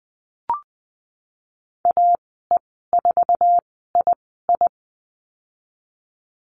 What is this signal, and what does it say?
Morse code, heard as a keyed tone, about 20 wpm, spaced as for 15 wpm, 700 Hz. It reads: AE4II
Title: 069 Lesson 12g - Letter d - Call Signs 20wpm